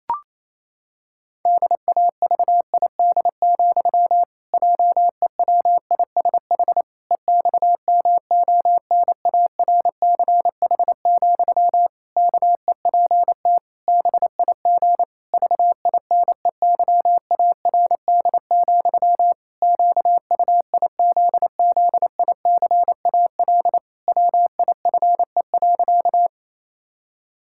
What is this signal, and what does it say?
Morse code, heard as a keyed tone, about 28 wpm, 700 Hz. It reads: DAVID, JEWISH EXMONARCH, KEPT BIG VINEYARD, QUIZZICAL WIFE.
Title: Pangram Challenge v2 - 28wpm